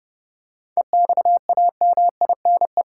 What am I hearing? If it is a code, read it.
Morse code, heard as a keyed tone, about 32 wpm, 700 Hz. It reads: EXAMINE
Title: Seven Letter Words - 30wpm